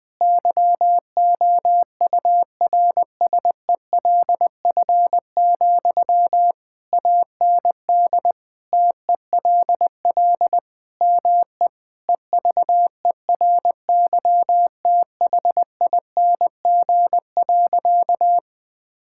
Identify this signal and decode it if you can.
Morse code, heard as a keyed tone, about 21 wpm, 700 Hz. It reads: YOURSELF, AND TELL ME EVERYTHING.
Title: Sentences from Top 1600 Words 20wpm